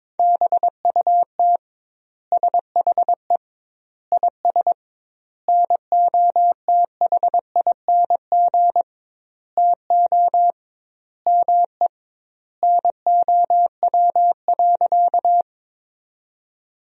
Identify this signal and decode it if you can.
Morse code, heard as a keyed tone, about 22 wpm, 700 Hz. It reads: BUT SHE IS NOTHING TO ME NOW.
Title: 2x Word-Spacing - Sentences from Top 500 Words - 22wpm